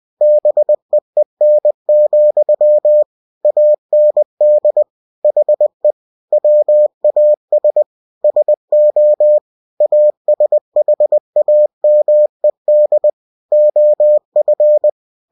Morse code, heard as a keyed tone, about 20 wpm, 600 Hz. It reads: BEEN, AND HE WAS SO ASHAMED OF
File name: cw-books / winnie-the-pooh / winnie-20wpm-page-0043